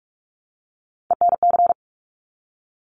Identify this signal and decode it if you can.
Morse code, heard as a keyed tone, about 46 wpm, 700 Hz. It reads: ENC
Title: Top Three Letter Combinations in Words 45wpm